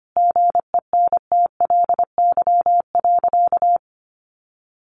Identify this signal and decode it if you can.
Morse code, heard as a keyed tone, about 25 wpm, 700 Hz. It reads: GENTLY.
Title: Sentences from Top 1500 Words 25wpm